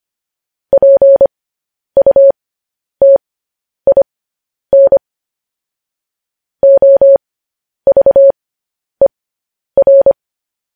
Morse code, heard as a keyed tone, about 25 wpm, spaced as for 10 wpm, 550 Hz. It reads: PUTIN OVER
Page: news-headlines-in-morse-code-at-10-wpm